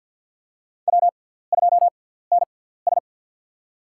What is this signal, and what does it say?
Morse code, heard as a keyed tone, about 51 wpm, spaced as for 17 wpm, 700 Hz. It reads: W2NS
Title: USA Call Signs - 4 Characters - 6x Character Spacing - 50wpm